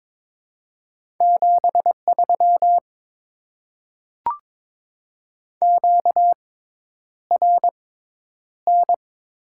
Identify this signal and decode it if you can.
Morse code, heard as a keyed tone, about 23 wpm, spaced as for 10 wpm, 700 Hz. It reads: 73 QRN
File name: Top 90 QSO Elements - 6x Character Spacing - 22wpm